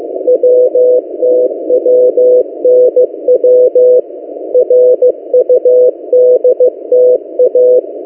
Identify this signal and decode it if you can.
Morse code, heard as a keyed tone, about 15 wpm, 500 Hz. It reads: WTWNW RUDTA
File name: M08a- -1355utc-8096khz